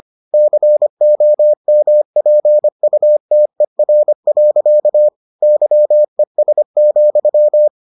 Morse code, heard as a keyed tone, about 25 wpm, 600 Hz. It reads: COMPUTER. YES,